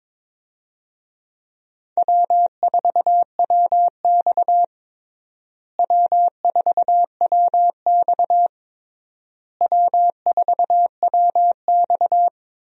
Morse code, heard as a keyed tone, about 22 wpm, 700 Hz. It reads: W4WX W4WX W4WX
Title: USA Call Signs - 4 Characters - Repeated 3x - 22wpm